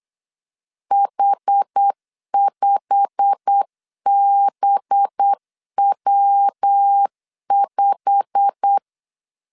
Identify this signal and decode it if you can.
Morse code, heard as a keyed tone, about 8 wpm, 800 Hz. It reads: H5BW5
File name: LEZIONI / 8 YJ45 / 8 YJ45 8wpm 3214